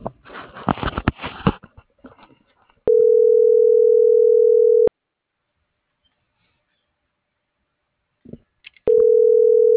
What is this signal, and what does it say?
Ringback tone (North America), 440 + 480 Hz.